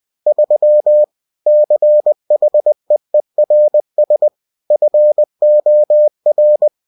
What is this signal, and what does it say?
Morse code, heard as a keyed tone, about 20 wpm, 600 Hz. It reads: 3 CHEERS FOR